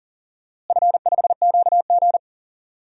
Morse code, heard as a keyed tone, about 40 wpm, 700 Hz. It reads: F5QG
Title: DXCC-100 Call Signs - 4 Characters - Repeated 2x - 40wpm